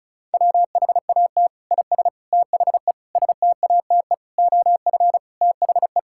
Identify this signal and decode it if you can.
Morse code, heard as a keyed tone, about 36 wpm, 700 Hz. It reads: WHAT IS THE STATE OF THE
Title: Sentences from Top 1000 Words - Review Part 5 - 35wpm